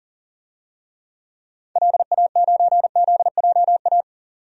Morse code, heard as a keyed tone, about 40 wpm, 700 Hz. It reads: LA9ZJA
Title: DXCC-100 Call Signs - 6 Characters - 40wpm